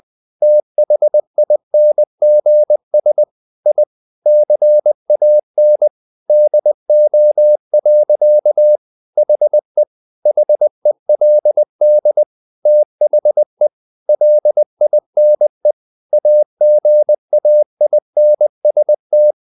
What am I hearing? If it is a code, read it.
Morse code, heard as a keyed tone, about 20 wpm, 600 Hz. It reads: THINGS I CAN DO. HE HELD THE LINE AGAINST
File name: the old man and the sea / the old man and the sea part 1 20wpm